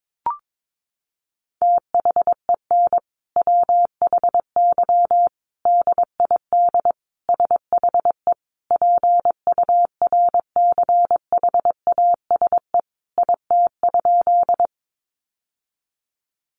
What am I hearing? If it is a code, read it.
Morse code, heard as a keyed tone, about 22 wpm, 700 Hz. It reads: THEN WHY DID SHE PURCHASE IT?
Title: Sentences from Top 800 Words 22wpm